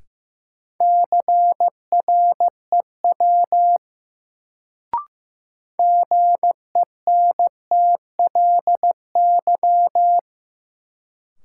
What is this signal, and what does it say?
Morse code, heard as a keyed tone, about 15 wpm, 700 Hz. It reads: CREW GENTLY
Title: Sentences from Top 1500 Words - New Word Drill Alt - 15wpm